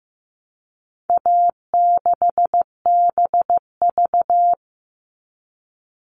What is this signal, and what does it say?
Morse code, heard as a keyed tone, about 15 wpm, 700 Hz. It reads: A6BV
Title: All Cut Numbers - 4-Digit 15wpm